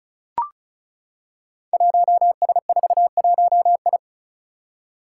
Morse code, heard as a keyed tone, about 36 wpm, 700 Hz. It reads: 1S41I
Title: Five Challenging Letter-Number Combinations 35wpm